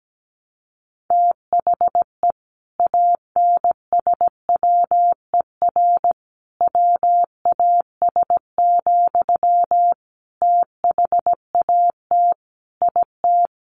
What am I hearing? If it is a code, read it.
Morse code, heard as a keyed tone, about 17 wpm, 700 Hz. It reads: THE ANSWER WAS, THAT IT
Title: Sentences from Top 1100 Words 17wpm